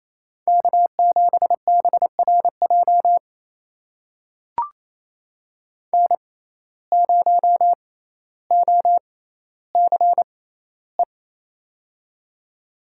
Morse code, heard as a keyed tone, about 28 wpm, spaced as for 14 wpm, 700 Hz. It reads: K7BRJ N0OCE
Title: USA Call Signs - 5 Characters - 6x Character Spacing - 28wpm